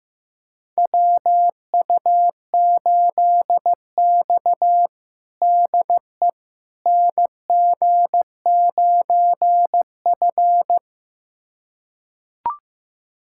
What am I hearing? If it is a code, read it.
Morse code, heard as a keyed tone, about 15 wpm, 700 Hz. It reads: WU8X DE NG9F
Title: USA Callsign DE Callsign - 4 Characters - 15wpm